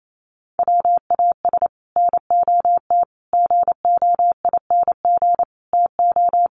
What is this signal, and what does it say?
Morse code, heard as a keyed tone, about 28 wpm, 700 Hz. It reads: WAS NOT GOING TO
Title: Sets of 5 Words Encore 28wpm